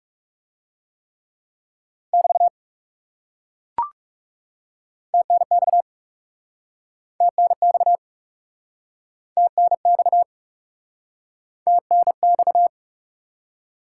Morse code, heard as a keyed tone, about 39 wpm, 700 Hz. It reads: <BT> TNX TNX TNX TNX